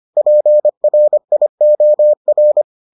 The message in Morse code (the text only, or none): PRIOR